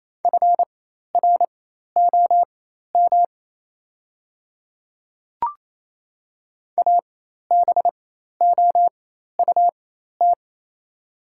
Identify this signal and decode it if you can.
Morse code, heard as a keyed tone, about 28 wpm, spaced as for 13 wpm, 700 Hz. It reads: FROM ABOUT
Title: 4x Character Spacing - Top 100 Words - 28wpm